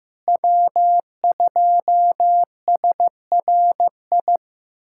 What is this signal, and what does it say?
Morse code, heard as a keyed tone, about 15 wpm, 700 Hz. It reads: W2SRI
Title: USA Callsign DE Callsign - 5 Characters - 15wpm